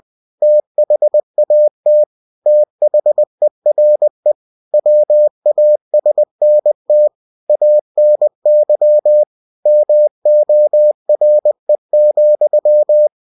Morse code, heard as a keyed tone, about 20 wpm, 600 Hz. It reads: THAT THERE WASNT ANY MORE,
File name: cw-books / winnie-the-pooh / winnie-20wpm-page-0036